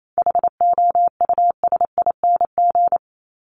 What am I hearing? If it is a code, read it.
Morse code, heard as a keyed tone, about 28 wpm, 700 Hz. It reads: HOUSING
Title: ING Suffix - Top 100 Words with Base Words 28wpm